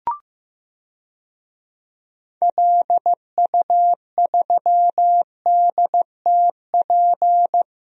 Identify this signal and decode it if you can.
Morse code, heard as a keyed tone, about 15 wpm, 700 Hz. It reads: LU3DTP
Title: DXCC-100 Call Signs - 6 Characters - Repeated 3x - 15wpm